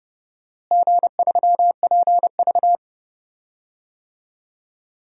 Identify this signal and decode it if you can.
Morse code, heard as a keyed tone, about 30 wpm, 700 Hz. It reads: G3PV
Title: DXCC-100 Call Signs - 4 Characters - Repeated 3x - 30wpm